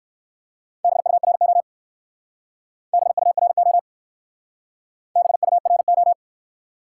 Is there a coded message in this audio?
Yes: Morse code from 0.8 to 6.1 s, heard as a keyed tone, about 56 wpm, 700 Hz. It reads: BURO BURO BURO